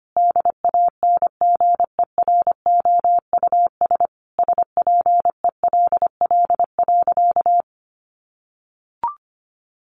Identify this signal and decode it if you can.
Morse code, heard as a keyed tone, about 25 wpm, 700 Hz. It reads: DANGEROUS SPELL.
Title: Sentences from Top 1200 Words 25wpm